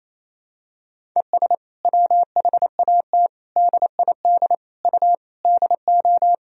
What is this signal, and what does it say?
Morse code, heard as a keyed tone, about 28 wpm, 700 Hz. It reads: ES WHAT DID U DO
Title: Sentences from Top 100 Words and Abbreviations 28wpm